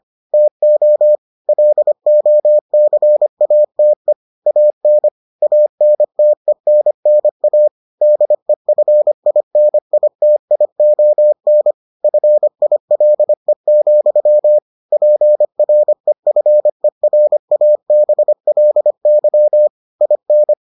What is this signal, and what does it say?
Morse code, heard as a keyed tone, about 25 wpm, 600 Hz. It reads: TO LOCATE AN ANTENNA DEFINITION FILE, PREFERABLY IN